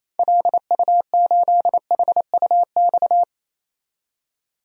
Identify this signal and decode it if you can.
Morse code, heard as a keyed tone, about 28 wpm, 700 Hz. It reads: LU8HUX
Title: DXCC-100 Call Signs - 6 Characters - 28wpm